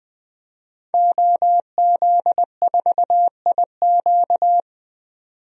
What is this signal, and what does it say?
Morse code, heard as a keyed tone, about 20 wpm, 700 Hz. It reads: OZ4IQ
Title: DXCC-100 Call Signs - 5 Characters - 20wpm